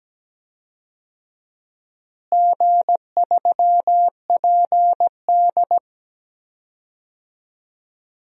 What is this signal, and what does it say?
Morse code, heard as a keyed tone, about 17 wpm, 700 Hz. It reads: G3PD
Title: DXCC-100 Call Signs - 4 Characters - Repeated 2x - 17wpm